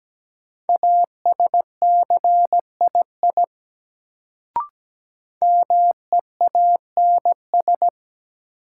Morse code, heard as a keyed tone, about 17 wpm, 700 Hz. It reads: ASCII MEANS
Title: Sets of 5 Words - New Word Drill - 17wpm